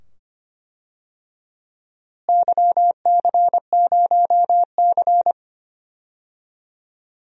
Morse code, heard as a keyed tone, about 25 wpm, 700 Hz. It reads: YC0C
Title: DXCC-100 Call Signs - 4 Characters - Repeated 2x - 25wpm